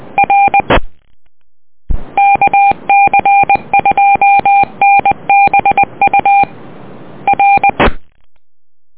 Morse code, heard as a keyed tone, about 20 wpm, 800 Hz. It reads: R KC2NBU R
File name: net audio